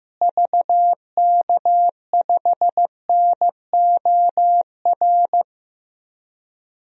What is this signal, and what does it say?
Morse code, heard as a keyed tone, about 15 wpm, 700 Hz. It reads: VK5NOR
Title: DXCC-100 Call Signs - 6 Characters - Repeated 3x - 15wpm